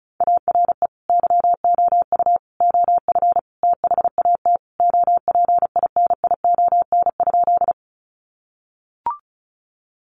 Morse code, heard as a keyed tone, about 34 wpm, 700 Hz. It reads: ARE YOU OF THAT OPINION?
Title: Sentences from Top 1200 Words 35wpm